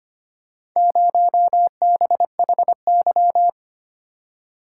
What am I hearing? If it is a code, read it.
Morse code, heard as a keyed tone, about 25 wpm, 700 Hz. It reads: 0BHY
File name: practice / 25wpm / Four Letter-Number Combinations - 25wpm